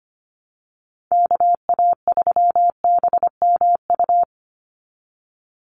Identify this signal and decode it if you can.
Morse code, heard as a keyed tone, about 25 wpm, 700 Hz. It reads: KA3BMU
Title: Call Signs Repeated 2x - 25wpm